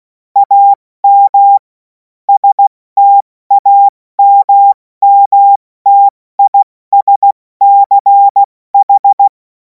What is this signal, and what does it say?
Morse code, heard as a keyed tone, about 16 wpm, 800 Hz. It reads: AM STAMMTISCH
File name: Morsekurs / Witze / Witz 24. Februar 2026 70 F80